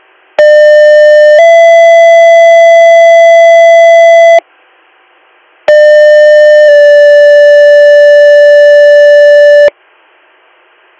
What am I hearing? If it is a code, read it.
Two-tone page, heard as a one-second tone, then a three-second tone, about 600 then 670 Hz, about 600 then 580 Hz. It is heard from 0.4 to 9.7 s.